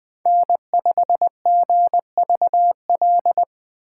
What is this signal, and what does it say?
Morse code, heard as a keyed tone, about 20 wpm, 700 Hz. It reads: N5GVL
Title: Speed Racing - USA 1x3 Call Signs 17wpm